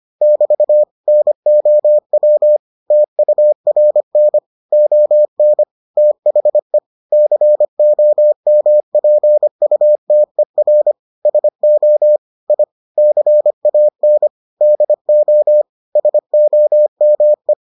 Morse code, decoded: <BT> NOW TURN ON THE COMPUTER SO I CAN DO SOME

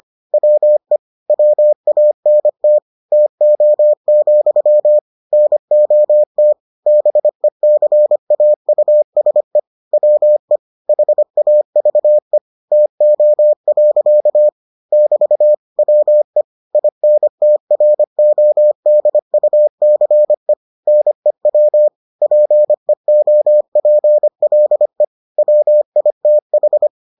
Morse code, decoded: WE WANT TO, NOT BECAUSE WE HAVE TO. <BT> WE INTRODUCE NEW PEOPLE WITH

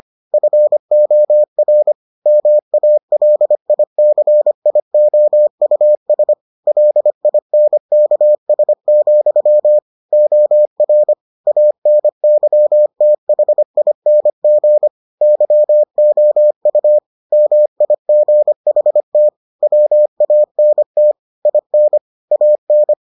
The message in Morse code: FOR MALICIOUS LINKS, OR ANYTHING YOU MIGHT WANT IN AN

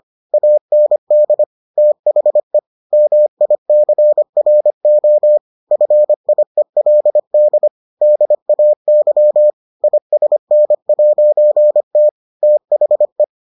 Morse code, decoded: AND THE MICRO FIELD DAY ISN'T THE

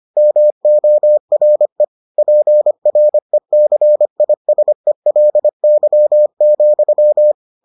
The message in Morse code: MORE PRECISELY,

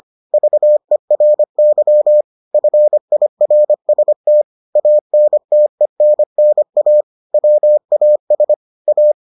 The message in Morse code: VERY FIRST ANTENNA WAS A

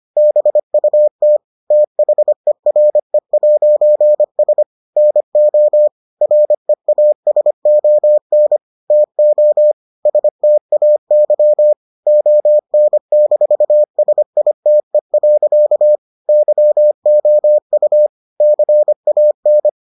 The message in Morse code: BUT THERE'S NO REASON TO STAY ON-SITE. YOU CAN

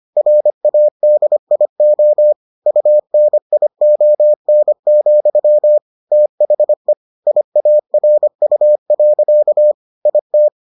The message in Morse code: RADIO UNION, THE IARU. IT

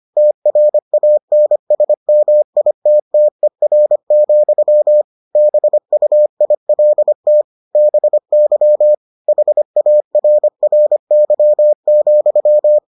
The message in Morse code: TRANSMITTER, BUILT BY HARRY,

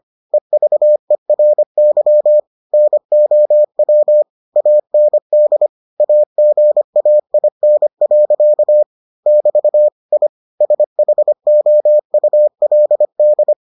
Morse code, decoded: EVERY NOW AND AGAIN. <BT> I SHOULD

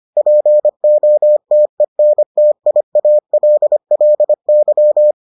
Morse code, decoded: POTENTIALLY

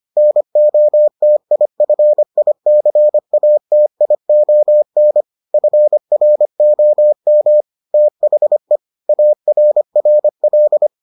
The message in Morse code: NOTIFICATION FROM THE ARRL

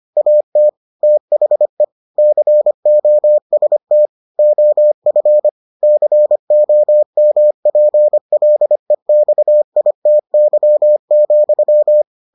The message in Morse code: AT THE COST OF COMPLEXITY,

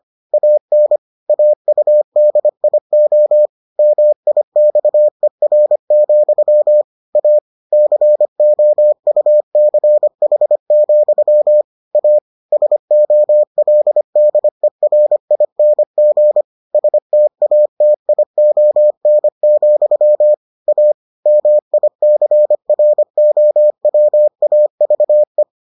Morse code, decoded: AN AUDIO MIXER, A COUCH, A SOLDERING STATION, A MICROWAVE

